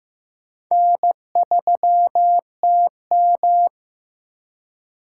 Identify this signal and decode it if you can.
Morse code, heard as a keyed tone, about 15 wpm, 700 Hz. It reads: N3TM
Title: USA Call Signs - 4 Characters - 3x Character Spacing - 15wpm